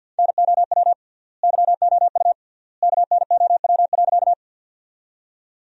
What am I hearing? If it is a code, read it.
Morse code, heard as a keyed tone, about 50 wpm, 700 Hz. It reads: NOW YOU KNOW.
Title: 3x Word-Spacing - Sentences from Top 100 Words - 50wpm